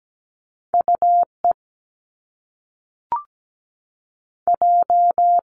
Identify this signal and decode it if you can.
Morse code, heard as a keyed tone, about 17 wpm, 700 Hz. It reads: UE J
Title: Top Two Letter Combinations in Words 17wpm